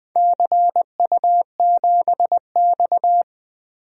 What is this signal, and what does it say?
Morse code, heard as a keyed tone, about 20 wpm, 700 Hz. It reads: CU7X
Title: DXCC-100 Call Signs - 4 Characters - Repeated 2x - 20wpm